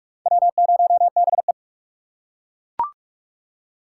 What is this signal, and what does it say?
Morse code, heard as a keyed tone, about 41 wpm, 700 Hz. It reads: W0BE
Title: USA Call Signs - 2x Character Spacing - 45wpm